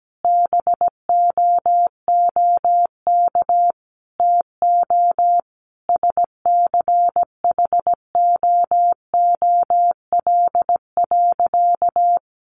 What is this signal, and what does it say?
Morse code, heard as a keyed tone, about 17 wpm, 700 Hz. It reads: BOOK TO SCHOOL.
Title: Sentences from Top 1400 Words 17wpm